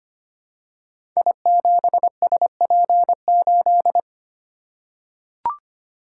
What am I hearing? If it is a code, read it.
Morse code, heard as a keyed tone, about 25 wpm, 700 Hz. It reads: I7SP8
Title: Code Groups - 5 Letter:Numbers - 25wpm